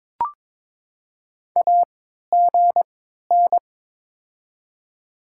Morse code, heard as a keyed tone, about 22 wpm, 700 Hz. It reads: A G N
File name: Top 20 QSO Elements - 3x Character Spacing - 22wpm